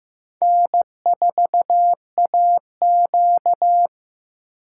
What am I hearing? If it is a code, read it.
Morse code, heard as a keyed tone, about 15 wpm, 700 Hz. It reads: N4AQ
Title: USA Call Signs - 4 Characters - 2x Character Spacing - 15wpm